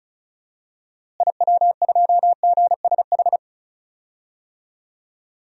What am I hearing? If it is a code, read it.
Morse code, heard as a keyed tone, about 34 wpm, 700 Hz. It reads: IW2GSH
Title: DXCC-100 Call Signs - 6 Characters - Repeated 3x - 35wpm